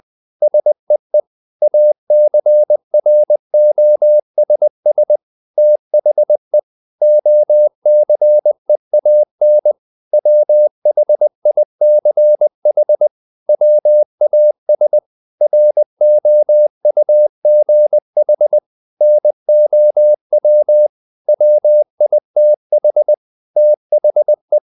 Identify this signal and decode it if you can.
Morse code, heard as a keyed tone, about 20 wpm, 600 Hz. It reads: SEE ACROSS THE OCEAN WHICH WAS ROUGH NOW WITH THE
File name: the old man and the sea part 2 20wpm